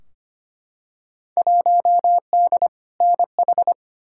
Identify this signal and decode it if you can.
Morse code, heard as a keyed tone, about 25 wpm, 700 Hz. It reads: 1D NH
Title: Field Day Exchanges Encore - 25wpm